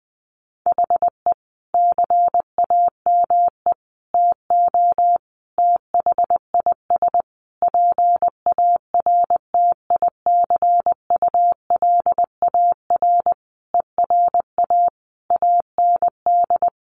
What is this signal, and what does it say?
Morse code, heard as a keyed tone, about 20 wpm, 700 Hz. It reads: HE CAME TO THIS PARTICULAR ERA AND
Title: A Sound Of Thunder 20wpm